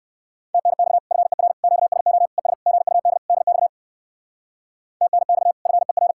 Morse code, heard as a keyed tone, about 59 wpm, 700 Hz. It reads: TNX FER QSO ES GUD DX TNX FER